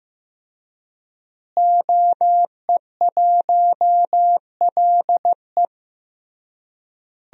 Morse code, heard as a keyed tone, about 16 wpm, 700 Hz. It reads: OE1LE
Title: DXCC-100 Call Signs - 5 Characters - Repeated 3x - 15wpm